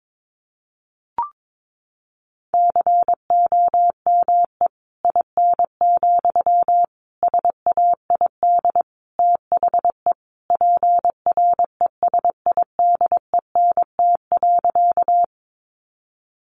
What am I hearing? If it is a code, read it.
Morse code, heard as a keyed tone, about 22 wpm, 700 Hz. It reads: COME IN, SAID THE PRESIDENT.
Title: Sentences from Top 1000 Words 22wpm